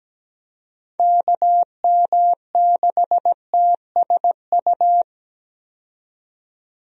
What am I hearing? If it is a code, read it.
Morse code, heard as a keyed tone, about 17 wpm, 700 Hz. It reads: KM6TSU
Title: USA Call Signs - 6 Characters - 17wpm